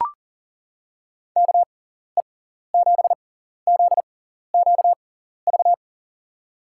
Morse code, heard as a keyed tone, about 40 wpm, spaced as for 14 wpm, 700 Hz. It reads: KE7ZQV
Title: USA Call Signs - 6 Characters - 6x Character Spacing - 40wpm